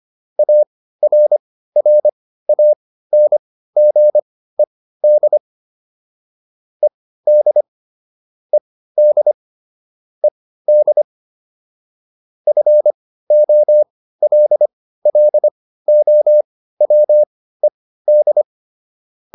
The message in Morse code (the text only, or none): ARRANGED ED ED ED FOLLOWED